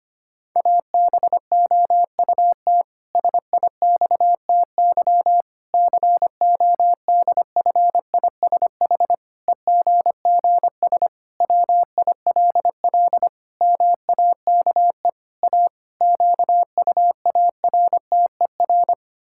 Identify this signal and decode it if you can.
Morse code, heard as a keyed tone, about 25 wpm, 700 Hz. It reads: ABOUT SIXTY CODFISH EGGS WILL MAKE A QUARTER
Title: Pangram Challenge v2 - 25wpm